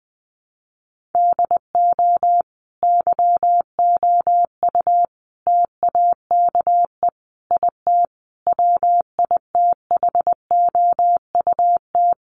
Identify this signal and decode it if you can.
Morse code, heard as a keyed tone, about 20 wpm, 700 Hz. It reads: DO YOU TAKE IT WITHOUT
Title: Sentences from Top 1200 Words 20wpm